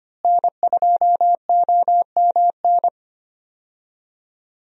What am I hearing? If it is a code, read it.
Morse code, heard as a keyed tone, about 25 wpm, 700 Hz. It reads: N2OMN
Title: USA Call Signs - 5 Characters - Repeated 3x - 25wpm